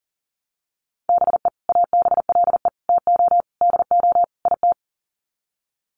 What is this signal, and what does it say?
Morse code, heard as a keyed tone, about 40 wpm, 700 Hz. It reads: BE ABLE TO DO IT
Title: Sets of 5 Words Encore2 40wpm